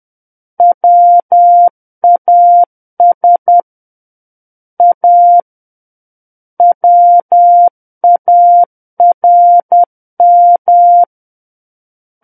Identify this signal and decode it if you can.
Morse code, heard as a keyed tone, about 10 wpm, 700 Hz. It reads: WAS A WARM
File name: Book One / 01-Chapter 1 - 10 WPM